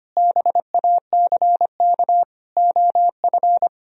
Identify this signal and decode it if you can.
Morse code, heard as a keyed tone, about 32 wpm, 700 Hz. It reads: BACK OF